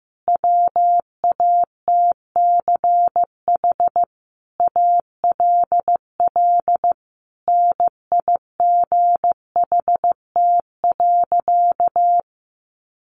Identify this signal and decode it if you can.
Morse code, heard as a keyed tone, about 15 wpm, 700 Hz. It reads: WATCH ALL NIGHT.
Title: Sentences from Top 300 Words 15wpm